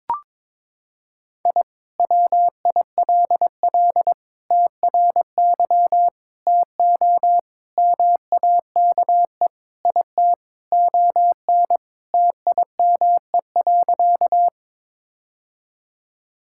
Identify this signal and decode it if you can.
Morse code, heard as a keyed tone, about 22 wpm, 700 Hz. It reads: I WILL TRY TO MAKE IT ON TIME.
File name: practice / 22wpm / Sentences with 3-Word Phrases - 22wpm